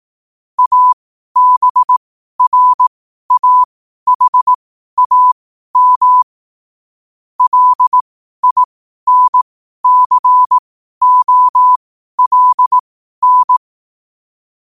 Morse code, heard as a keyed tone, about 18 wpm, spaced as for 12 wpm, 1000 Hz. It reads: ABRAHAM LINCOLN
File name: qotdfiles / 12wpm